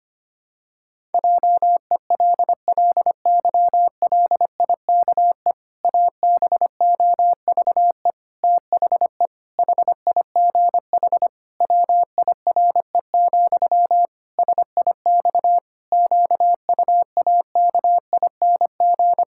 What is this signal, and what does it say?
Morse code, heard as a keyed tone, about 25 wpm, 700 Hz. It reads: JELLYLIKE ABOVE THE HIGH WIRE, SIX QUAKING
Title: Pangram Challenge v2 - 25wpm